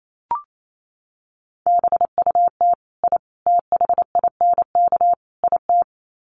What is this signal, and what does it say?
Morse code, heard as a keyed tone, about 28 wpm, 700 Hz. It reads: BUT I THINK IT
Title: Sets of 4 Words 28wpm